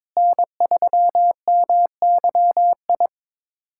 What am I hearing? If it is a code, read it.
Morse code, heard as a keyed tone, about 22 wpm, 700 Hz. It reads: N3MYI
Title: USA Callsign DE Callsign - 5 Characters - 22wpm